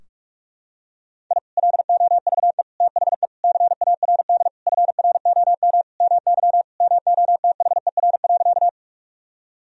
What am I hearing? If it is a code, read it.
Morse code, heard as a keyed tone, about 46 wpm, 700 Hz. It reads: I LOVE THE CARD FROM MY MOTHER.
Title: Sentences from Top 1000 Words - Review Part 4 - 45wpm